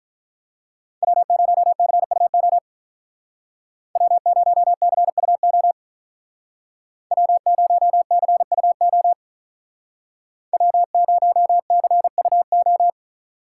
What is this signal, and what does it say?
Morse code, heard as a keyed tone, about 43 wpm, 700 Hz. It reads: W0CUO W0CUO W0CUO W0CUO
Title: Speed Racing - USA 1x3 Call Signs 35wpm